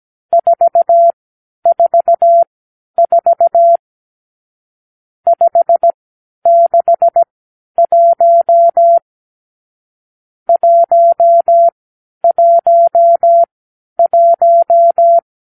Morse code, heard as a keyed tone, about 17 wpm, spaced as for 10 wpm, 680 Hz. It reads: 444 561 111
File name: k7qo / Lesson 50 (1-6)